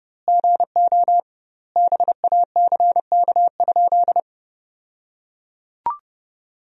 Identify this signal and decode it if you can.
Morse code, heard as a keyed tone, about 32 wpm, 700 Hz. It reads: GO BACK?
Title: 2x Word-Spacing - Sentences from Top 400 Words - 30wpm